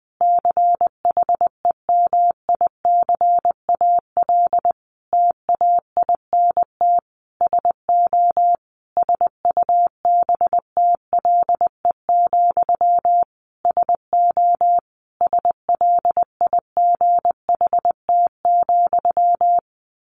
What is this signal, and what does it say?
Morse code, heard as a keyed tone, about 20 wpm, 700 Hz. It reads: CHEMICAL TAINT SO SUBTLE, SO SLIGHT,
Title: A Sound Of Thunder 20wpm